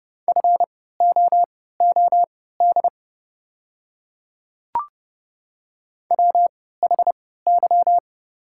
Morse code, heard as a keyed tone, about 30 wpm, spaced as for 17 wpm, 700 Hz. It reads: FOOD WHY
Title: 3x Character Spacing - Top 500 Words - 30wpm